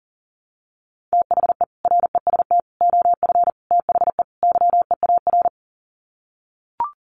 Morse code, heard as a keyed tone, about 40 wpm, 700 Hz. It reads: THE REST OF THE YEAR
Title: Sets of 5 Words Encore 40wpm